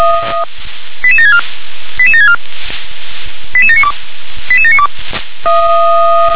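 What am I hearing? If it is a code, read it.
Five-tone selective call (ZVEI), heard as quick runs of five tones, tones of about 70 ms. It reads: 88764 88764 88792 88792
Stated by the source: ZVEI five-tone call 88727 received by SDR.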